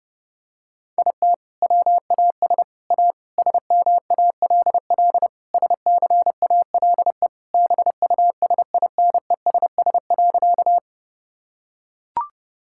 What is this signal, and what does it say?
Morse code, heard as a keyed tone, about 32 wpm, 700 Hz. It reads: IT WAS A SMALL SCALE BUSINESS.
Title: Sentences from Top 1000 Words - Review Part 5 - 30wpm